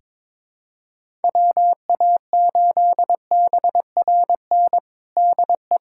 Morse code, heard as a keyed tone, about 22 wpm, 700 Hz. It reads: WA8BRN DE
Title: USA Callsign DE Callsign - 6 Characters - 22wpm